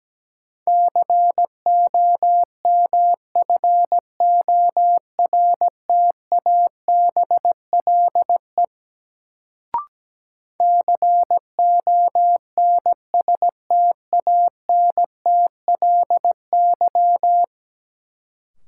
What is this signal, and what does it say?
Morse code, heard as a keyed tone, about 17 wpm, 700 Hz. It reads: COMFORTABLE CONSTANTLY
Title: Sentences from Top 1000 Words - Review Part 7 - New Word Drill - 17wpm